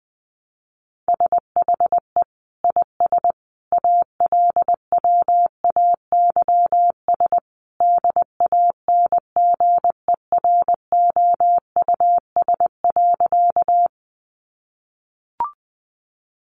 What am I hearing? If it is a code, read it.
Morse code, heard as a keyed tone, about 20 wpm, 700 Hz. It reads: SHE IS ALWAYS DANGEROUS.
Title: Sentences from Top 1200 Words 20wpm